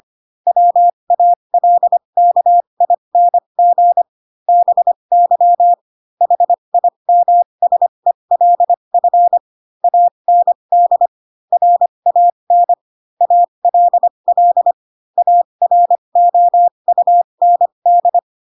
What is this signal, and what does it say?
Morse code, heard as a keyed tone, about 25 wpm, spaced as for 22 wpm, 700 Hz. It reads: WALKING BY HIMSELF AND RAN ALL AROUND
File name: Pinocchio / Pinocchio ch3 18 25 ik1hge